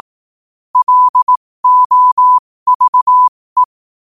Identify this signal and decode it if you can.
Morse code, heard as a keyed tone, about 18 wpm, spaced as for 15 wpm, 1000 Hz. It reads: LOVE
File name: qotdfiles / 15wpm